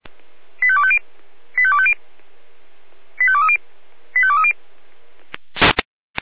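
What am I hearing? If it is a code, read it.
Five-tone selective call (ZVEI), heard as quick runs of five tones, tones of about 70 ms. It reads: 85250 85250 85230 85230
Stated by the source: Five-tone ZVEI selective call received on an SDR.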